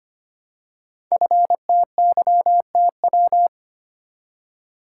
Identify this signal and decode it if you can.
Morse code, heard as a keyed tone, about 25 wpm, 700 Hz. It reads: FTYTW
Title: Code Groups - 5 Letters - 25wpm